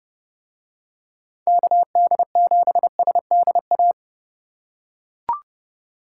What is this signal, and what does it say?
Morse code, heard as a keyed tone, about 29 wpm, 700 Hz. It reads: KD7SDA